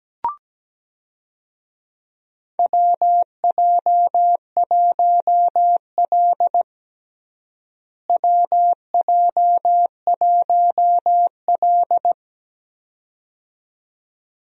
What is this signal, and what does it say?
Morse code, heard as a keyed tone, about 17 wpm, 700 Hz. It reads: WJ1L WJ1L
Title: USA Call Signs - 4 Characters - Repeated 2x - 17wpm